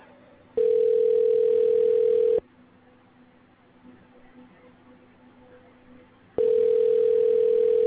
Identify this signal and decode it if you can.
Ringback tone (North America), 440 + 480 Hz.
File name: recordings / RE6100ae5122b f33c8389a